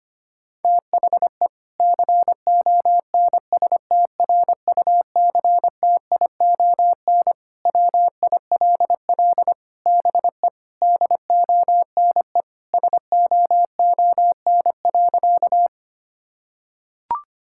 Morse code, heard as a keyed tone, about 25 wpm, 700 Hz. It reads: THE CONSTRUCTION WILL BE DONE SOON.
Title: Sentences from Top 1600 Words 25wpm